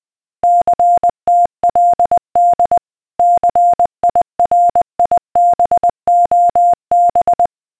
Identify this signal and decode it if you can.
Morse code, heard as a keyed tone, about 20 wpm, 700 Hz. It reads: CTLD CIRIBOB
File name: DCS-CSAR / beacon